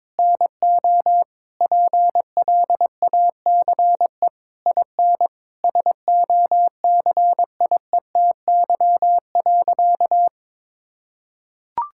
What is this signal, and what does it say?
Morse code, heard as a keyed tone, about 22 wpm, 700 Hz. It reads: NO PLACE IN SOCIETY.
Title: Sentences from Top 1600 Words 22wpm